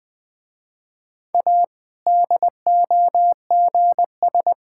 Morse code, heard as a keyed tone, about 21 wpm, 700 Hz. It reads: A DOGS
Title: Difficult Sentences with Hints - 20wpm